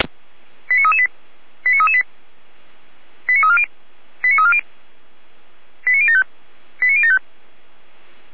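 Five-tone selective call (ZVEI), heard as quick runs of five tones, tones of about 70 ms. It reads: 89338 89338 89340 89340 89075 89075
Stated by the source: ZVEI five-tone selective call 89319 received on an SDR, archived by the Feuerwehr Orlamünde fire brigade.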